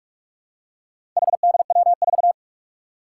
Morse code, heard as a keyed tone, about 44 wpm, 700 Hz. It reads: HDW4
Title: Four Challenging Letter-Number Combinations 45wpm